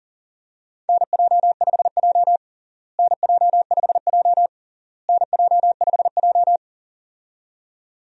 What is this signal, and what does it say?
Morse code, heard as a keyed tone, about 37 wpm, 700 Hz. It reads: NJ5J NJ5J NJ5J
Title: USA Call Signs - 4 Characters - Repeated 3x - 40wpm